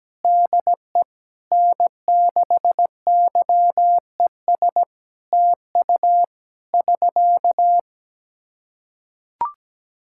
Morse code, heard as a keyed tone, about 17 wpm, 700 Hz. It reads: DE N6YES TU <SK>
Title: Practice QSO - Exchange 4 - Ending - 17wpm